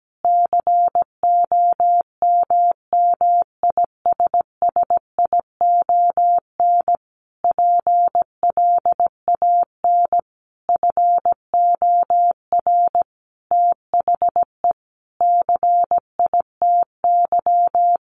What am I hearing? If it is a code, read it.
Morse code, heard as a keyed tone, about 17 wpm, 700 Hz. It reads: COMMISSION PLAN FOR THE CITY
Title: Sentences from Top 1400 Words 17wpm